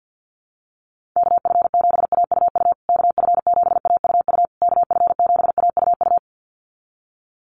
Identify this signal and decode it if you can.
Morse code, heard as a keyed tone, about 48 wpm, 700 Hz. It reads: KF7AUU KF7AUU KF7AUU
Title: USA Call Signs Repeated 3x 50wpm